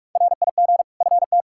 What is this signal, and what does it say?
Morse code, heard as a keyed tone, about 46 wpm, 700 Hz. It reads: RIG FT
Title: Practice QSO - Exchange 2 - Info - 45wpm